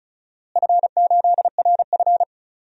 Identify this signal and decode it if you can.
Morse code, heard as a keyed tone, about 34 wpm, 700 Hz. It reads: F8RF
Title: DXCC-100 Call Signs - 4 Characters - Repeated 2x - 35wpm